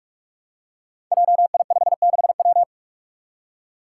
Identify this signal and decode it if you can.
Morse code, heard as a keyed tone, about 41 wpm, 700 Hz. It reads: JI56W
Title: Five Challenging Letter-Number Combinations 45wpm